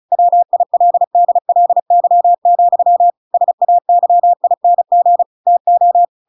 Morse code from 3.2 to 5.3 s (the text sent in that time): SAYING